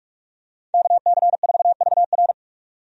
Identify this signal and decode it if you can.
Morse code, heard as a keyed tone, about 41 wpm, 700 Hz. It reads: KC4VR
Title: USA Call Signs - 5 Characters - 45wpm